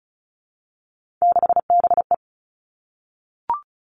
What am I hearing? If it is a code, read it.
Morse code, heard as a keyed tone, about 34 wpm, 700 Hz. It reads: 6BE
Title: Three Challenging Letter-Number Combinations 35wpm